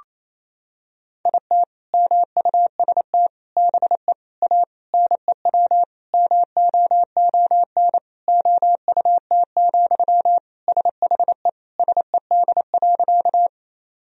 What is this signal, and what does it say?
Morse code, heard as a keyed tone, about 27 wpm, 700 Hz. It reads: IT MUST BE A NEW MOON OUT, SHE SED.
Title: Sentences from Top 500 Words and Abbreviations 28wpm